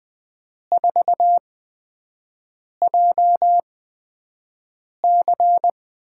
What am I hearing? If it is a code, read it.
Morse code, heard as a keyed tone, about 20 wpm, spaced as for 5 wpm, 700 Hz. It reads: 4JC
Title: USA Call Signs - 5 Characters - 8x Character Spacing - 20wpm